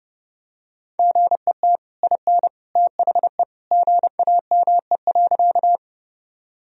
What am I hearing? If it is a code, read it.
Morse code, heard as a keyed tone, about 31 wpm, 700 Hz. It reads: GET IN THE GAME.
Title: Sentences from Top 1000 Words - Review Part 3 - 30wpm